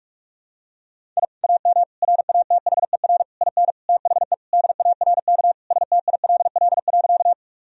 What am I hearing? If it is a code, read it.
Morse code, heard as a keyed tone, about 46 wpm, 700 Hz. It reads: I AM RATHER IN THE DARK STILL.
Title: Sentences from Top 1000 Words - Review Part 4 - 45wpm